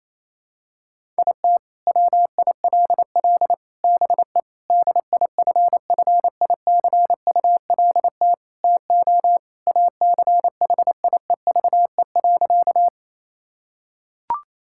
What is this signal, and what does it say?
Morse code, heard as a keyed tone, about 28 wpm, 690 Hz. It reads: IT WILL BE DIFFICULT TO ACHIEVE.
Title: Sentences from Top 1000 Words - Review Part 6 - 28wpm